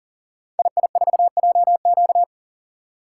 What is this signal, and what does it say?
Morse code, heard as a keyed tone, about 39 wpm, 700 Hz. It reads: II4JQ
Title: Code Groups - 5 Letter:Numbers - 40wpm